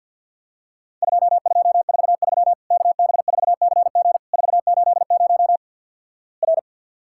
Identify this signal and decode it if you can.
Morse code, heard as a keyed tone, about 50 wpm, 700 Hz. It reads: 2243 KB4CG 480 R
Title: Worked All Europe DX Contest - QTC Exchange - 50wpm